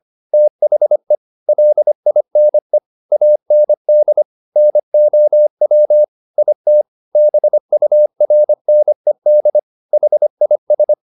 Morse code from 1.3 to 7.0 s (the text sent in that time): LINE AND NOW IT